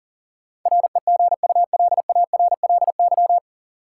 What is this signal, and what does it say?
Morse code, heard as a keyed tone, about 40 wpm, 700 Hz. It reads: REGULARLY
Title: Sentences from Top 1400 Words - New Word Drill - 40wpm